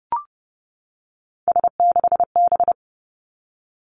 Morse code, heard as a keyed tone, about 30 wpm, 700 Hz. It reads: S6B
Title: Three Challenging Letter-Number Combinations 30wpm